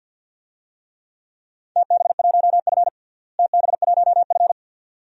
Morse code, heard as a keyed tone, about 49 wpm, 700 Hz. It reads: TB1F TB1F
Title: DXCC-100 Call Signs - 4 Characters - Repeated 2x - 50wpm